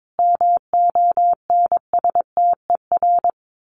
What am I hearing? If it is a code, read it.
Morse code, heard as a keyed tone, about 22 wpm, 700 Hz. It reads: MONSTER
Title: A Sound Of Thunder 22wpm